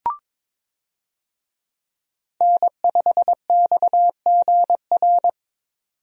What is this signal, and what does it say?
Morse code, heard as a keyed tone, about 22 wpm, 700 Hz. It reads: N5XGR
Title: USA Call Signs - 5 Characters - 22wpm